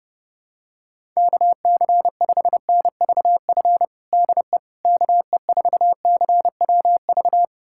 Morse code, heard as a keyed tone, about 32 wpm, 700 Hz. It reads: KC5NVF DE KE4CWV
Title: USA Callsign DE Callsign 30wpm